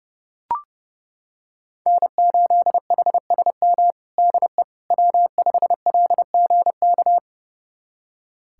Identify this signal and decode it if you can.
Morse code, heard as a keyed tone, about 29 wpm, 700 Hz. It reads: N8HSM DE W5LGK
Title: USA Callsign DE Callsign - 5 Characters - 30wpm